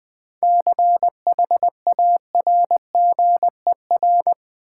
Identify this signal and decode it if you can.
Morse code, heard as a keyed tone, about 21 wpm, 700 Hz. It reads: CHARGER
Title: Easy Words with Hints - 20wpm